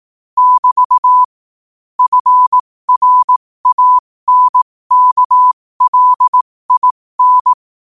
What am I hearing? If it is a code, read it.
Morse code, heard as a keyed tone, about 18 wpm, spaced as for 15 wpm, 1000 Hz. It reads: <BT> FRANKLIN
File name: qotdfiles / 15wpm